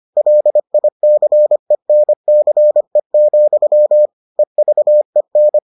Morse code, heard as a keyed tone, about 25 wpm, 600 Hz. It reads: LICENCE, EVEN